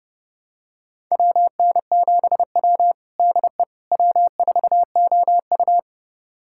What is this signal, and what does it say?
Morse code, heard as a keyed tone, about 30 wpm, 700 Hz. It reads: WN7W DE W4OU
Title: USA Callsign DE Callsign - 4 Characters - 30wpm